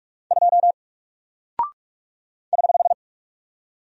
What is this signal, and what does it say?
Morse code, heard as a keyed tone, about 43 wpm, 700 Hz. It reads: 2 <HH>
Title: Single Character (Letter:Number:Punctuation:Prosign) Alt 45wpm